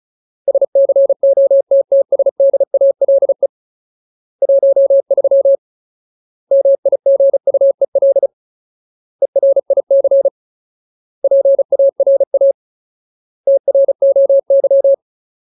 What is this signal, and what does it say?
Morse code, heard as a keyed tone, about 35 wpm, 550 Hz. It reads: SCOTTSDALE 13 MIGUEL ERIC PARA TROY
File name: Ultimate Word List / 35WPM Ultimate